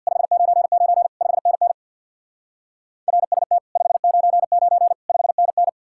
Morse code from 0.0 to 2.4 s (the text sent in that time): 599 5NN